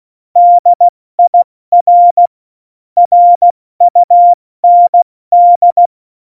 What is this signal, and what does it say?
Morse code, heard as a keyed tone, about 16 wpm, 700 Hz. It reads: DIR RUND
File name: Morsekurs / Witze / Witz 14 April 2020, 70 F80